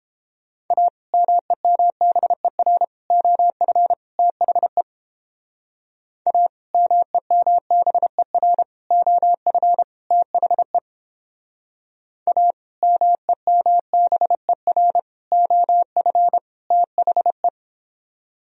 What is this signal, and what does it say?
Morse code, heard as a keyed tone, about 29 wpm, 700 Hz. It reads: A MEMBER OF THE A MEMBER OF THE A MEMBER OF THE